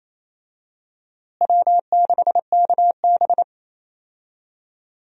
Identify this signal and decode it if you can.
Morse code, heard as a keyed tone, about 28 wpm, 700 Hz. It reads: W6KB
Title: USA Call Signs - 4 Characters - Repeated 3x - 28wpm